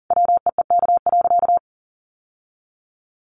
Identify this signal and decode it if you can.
Morse code, heard as a keyed tone, about 40 wpm, 700 Hz. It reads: WEEK.
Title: Sentences from Top 1200 Words 40wpm